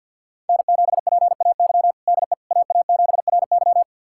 Morse code, heard as a keyed tone, about 51 wpm, 700 Hz. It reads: N7PAY DE AA7RY
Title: USA Callsign DE Callsign - 5 Characters - 50wpm